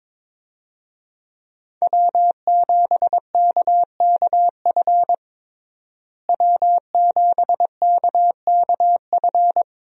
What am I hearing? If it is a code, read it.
Morse code, heard as a keyed tone, about 22 wpm, 700 Hz. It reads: W7KKF W7KKF
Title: USA Call Signs - 5 Characters - Repeated 2x - 22wpm